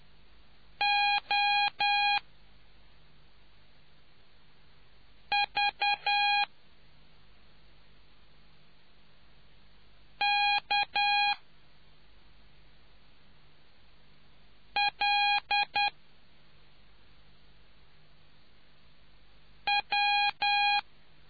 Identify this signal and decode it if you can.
Morse code, heard as a keyed tone, about 10 wpm, spaced as for 2 wpm, 820 Hz. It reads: OVKLW